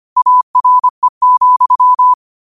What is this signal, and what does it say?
Morse code, heard as a keyed tone, about 25 wpm, 1000 Hz. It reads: ARE,
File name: qotdfiles / 25wpm